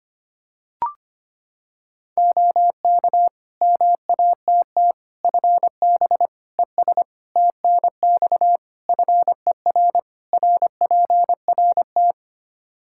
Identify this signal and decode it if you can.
Morse code, heard as a keyed tone, about 25 wpm, 700 Hz. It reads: OK MATT FB ES TNX FER RPRT
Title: Practice QSO - Exchange 2 - Info - 25wpm